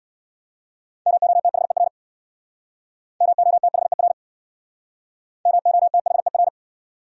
Morse code, heard as a keyed tone, about 63 wpm, 700 Hz. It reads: MOTHER MOTHER MOTHER